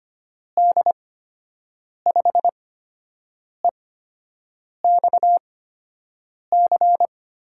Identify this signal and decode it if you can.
Morse code, heard as a keyed tone, about 25 wpm, spaced as for 7 wpm, 700 Hz. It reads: D5EXC
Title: USA Call Signs - 6 Characters - 8x Character Spacing - 25wpm